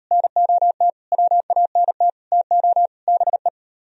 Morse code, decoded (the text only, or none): NOT WANT TO BE